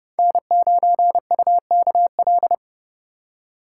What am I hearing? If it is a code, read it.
Morse code, heard as a keyed tone, about 29 wpm, 700 Hz. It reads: N9UKL